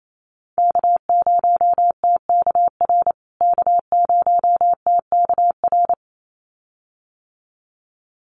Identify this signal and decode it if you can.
Morse code, heard as a keyed tone, about 28 wpm, 700 Hz. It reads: K0TKR K0TKR
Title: USA Call Signs Repeated 2x 28wpm